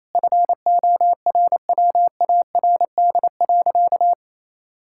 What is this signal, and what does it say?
Morse code, heard as a keyed tone, about 28 wpm, 700 Hz. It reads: FORWARD.
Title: Pangram Challenge v2 - 28wpm